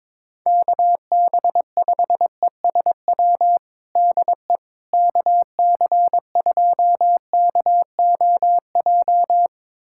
Morse code, heard as a keyed tone, about 22 wpm, 700 Hz. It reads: KB5ESW DE KC2KOJ
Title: USA Callsign DE Callsign 22wpm